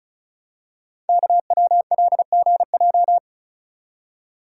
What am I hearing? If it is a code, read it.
Morse code, heard as a keyed tone, about 36 wpm, 700 Hz. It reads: KWLGJ
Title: Code Groups - 5 Letters - 35wpm